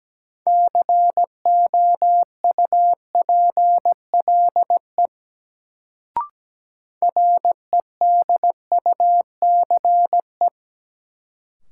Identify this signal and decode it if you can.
Morse code, heard as a keyed tone, about 17 wpm, 700 Hz. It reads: COUPLE REDUCE
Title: Sentences from Top 1100 Words - New Word Drill Alt - 17wpm